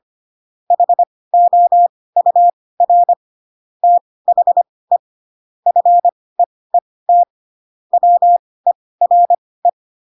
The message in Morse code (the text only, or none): HOUR THE FEET WERE